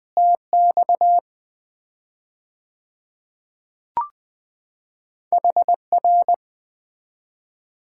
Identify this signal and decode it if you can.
Morse code, heard as a keyed tone, about 20 wpm, 700 Hz. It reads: TX HR
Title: Common QSO Phrases - 4x Word-Spacing - 20wpm